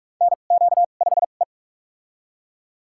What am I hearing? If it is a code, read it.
Morse code, heard as a keyed tone, about 47 wpm, spaced as for 33 wpm, 700 Hz. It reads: NQ5E
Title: USA Call Signs - 2x Character Spacing - 45wpm